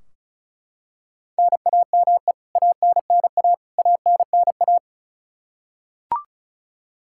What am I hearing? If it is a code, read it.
Morse code, heard as a keyed tone, about 34 wpm, 700 Hz. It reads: NAME ANNA ANNA
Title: Practice QSO - Exchange 1 - Beginning - 35wpm